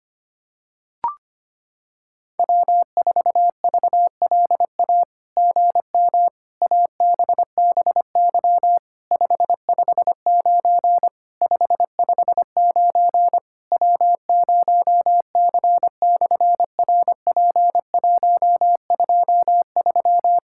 Morse code, heard as a keyed tone, about 25 wpm, 700 Hz. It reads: W4VLA GM ABBY 559 559 W0C/RP123
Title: SOTA QSO Practice 25wpm